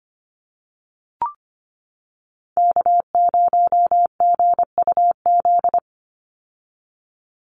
Morse code, heard as a keyed tone, about 25 wpm, 700 Hz. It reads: K0GUZ
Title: Call Signs - 25wpm